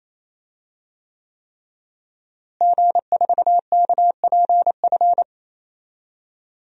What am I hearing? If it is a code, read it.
Morse code, heard as a keyed tone, about 28 wpm, 700 Hz. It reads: G4KPF
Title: DXCC-100 Call Signs - 5 Characters - Repeated 3x - 28wpm